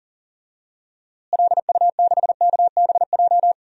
Morse code, heard as a keyed tone, about 40 wpm, 700 Hz. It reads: LU6KBJ
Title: DXCC-100 Call Signs - 6 Characters - Repeated 3x - 40wpm